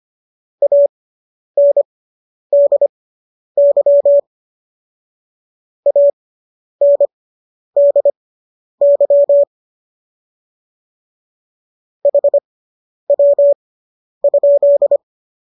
Morse code, heard as a keyed tone, about 25 wpm, spaced as for 10 wpm, 570 Hz. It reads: ANDY ANDY HW?